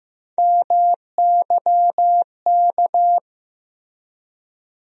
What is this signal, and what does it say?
Morse code, heard as a keyed tone, about 15 wpm, 700 Hz. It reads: MYK
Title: USA Call Sign Postfixes - 3 Letters 15wpm